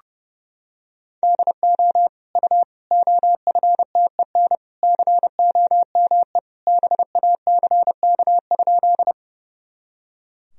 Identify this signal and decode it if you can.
Morse code, heard as a keyed tone, about 30 wpm, 700 Hz. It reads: DO U OFTEN COME BACK?
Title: Sentences from Top 300 Words and Abbreviations 30wpm